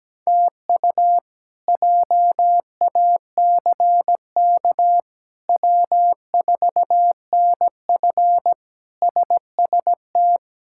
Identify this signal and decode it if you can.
Morse code, heard as a keyed tone, about 17 wpm, 700 Hz. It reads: TU JACK W4NF SST
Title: K1USN SST QSO Practice - 17wpm